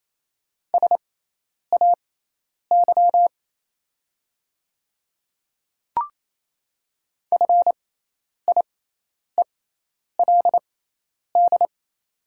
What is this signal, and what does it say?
Morse code, heard as a keyed tone, about 27 wpm, spaced as for 10 wpm, 690 Hz. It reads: SAY FIELD
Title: 6x Character Spacing - Top 500 Words - 28wpm